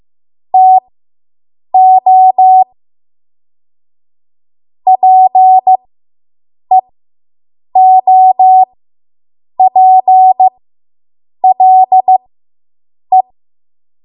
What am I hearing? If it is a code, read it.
Morse code, heard as a keyed tone, about 15 wpm, spaced as for 7 wpm, 750 Hz. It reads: TO PEOPLE